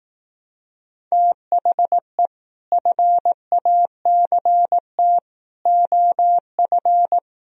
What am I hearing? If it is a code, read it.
Morse code, heard as a keyed tone, about 18 wpm, 700 Hz. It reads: THE FACT OF
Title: Speed-Racing - Sets of 5 Words 15wpm